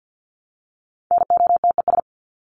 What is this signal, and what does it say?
Morse code, heard as a keyed tone, about 50 wpm, 700 Hz. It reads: NOTES
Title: ES:S Suffix - Top 100 Words with Base Words 50wpm